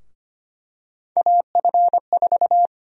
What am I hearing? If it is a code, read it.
Morse code, heard as a keyed tone, about 25 wpm, 700 Hz. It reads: AF4
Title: USA Call Sign Prefixes - Repeated 2x - 25wpm